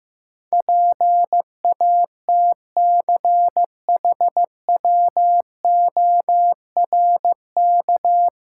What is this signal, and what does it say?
Morse code, heard as a keyed tone, about 15 wpm, 700 Hz. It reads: PATCHWORK
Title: Pangram Challenge v2 - 15wpm